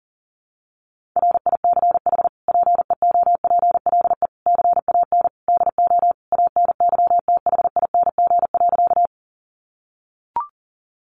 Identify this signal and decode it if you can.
Morse code, heard as a keyed tone, about 40 wpm, 700 Hz. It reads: RICH PEOPLE CAN DO ANYTHING.
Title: Sentences from Top 1100 Words 40wpm